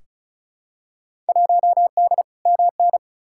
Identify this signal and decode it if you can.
Morse code, heard as a keyed tone, about 36 wpm, 700 Hz. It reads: 1D MN
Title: Field Day Exchanges - 35wpm